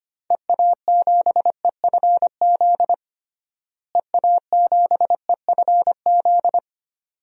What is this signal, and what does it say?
Morse code, heard as a keyed tone, about 25 wpm, 700 Hz. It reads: EA7EFZ EA7EFZ
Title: DXCC-100 Call Signs - 6 Characters - Repeated 2x - 25wpm